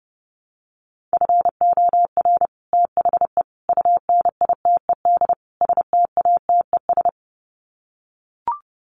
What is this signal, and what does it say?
Morse code, heard as a keyed tone, about 30 wpm, 700 Hz. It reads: FOR THE UNITED STATES
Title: Sets of 4 Words 30wpm